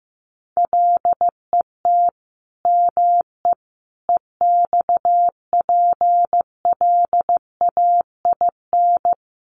Morse code, heard as a keyed tone, about 15 wpm, 700 Hz. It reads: LET ME EXPLAIN
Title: Sentences from Top 1100 Words 15wpm